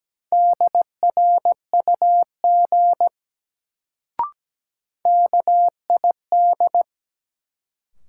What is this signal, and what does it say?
Morse code, heard as a keyed tone, about 17 wpm, 700 Hz. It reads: DRUG KID
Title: Sets of 5 Words - New Word Drill - 17wpm